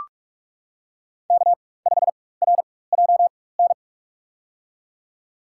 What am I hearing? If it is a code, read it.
Morse code, heard as a keyed tone, about 45 wpm, spaced as for 21 wpm, 700 Hz. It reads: K5RJN
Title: USA Call Signs - 5 Characters - 4x Character Spacing - 45wpm